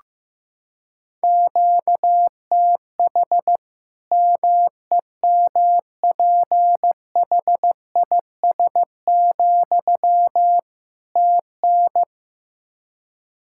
Morse code, heard as a keyed tone, about 15 wpm, 700 Hz. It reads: QTH MEMPHIS, TN
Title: Practice QSO - Exchange 1 - Beginning - 15wpm